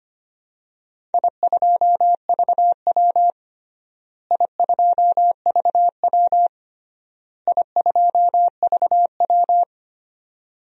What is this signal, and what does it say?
Morse code, heard as a keyed tone, about 25 wpm, 700 Hz. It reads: I2VW I2VW I2VW
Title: DXCC-100 Call Signs - 4 Characters - Repeated 3x - 25wpm